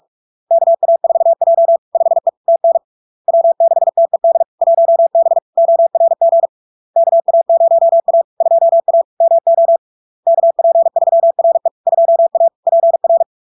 Morse code, heard as a keyed tone, about 46 wpm, 670 Hz. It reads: KA4J 5E TN W6TED 1B ORG KA0A 2A MO KP3RE 2A PR
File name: LARK-FD-2022-Log-in-CW-45wpm-670-Hz